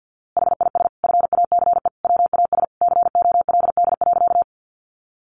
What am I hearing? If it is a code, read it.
Morse code, heard as a keyed tone, about 50 wpm, 700 Hz. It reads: HIS FACE WAS COLD.
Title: A Sound Of Thunder 50wpm